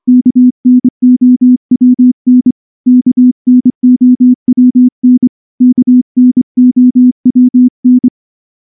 Morse code, heard as a keyed tone, about 24 wpm, 250 Hz. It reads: KNOWN KNOWN KNOWN
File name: morse / morse-training-25WPM